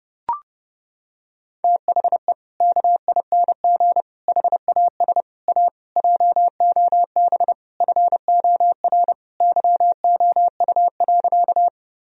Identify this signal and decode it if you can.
Morse code, heard as a keyed tone, about 32 wpm, 700 Hz. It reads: THE KING HAS A JOB FOR YOU.
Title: Sentences from Top 1000 Words - Review Part 3 - 30wpm